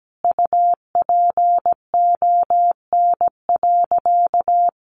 Morse code, heard as a keyed tone, about 17 wpm, 700 Hz. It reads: UPON.
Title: Sentences from Top 1100 Words 17wpm